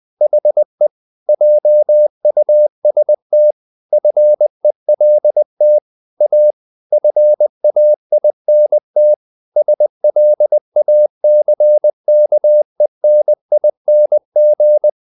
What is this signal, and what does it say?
Morse code, heard as a keyed tone, about 20 wpm, 600 Hz. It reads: HE JUST FELT A FAINT SLACKENING
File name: the old man and the sea part 2 20wpm